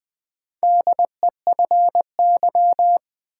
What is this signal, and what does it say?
Morse code, heard as a keyed tone, about 20 wpm, 700 Hz. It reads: DEFY
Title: Build A Word - 2 Letters At A Time - 4-Letter Words - 20wpm